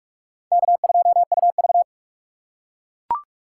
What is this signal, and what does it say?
Morse code, heard as a keyed tone, about 44 wpm, 700 Hz. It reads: K2UV